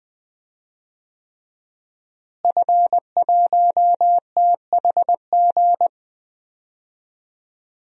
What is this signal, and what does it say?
Morse code, heard as a keyed tone, about 21 wpm, 700 Hz. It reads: F1THG
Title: DXCC-100 Call Signs - 5 Characters - Repeated 3x - 20wpm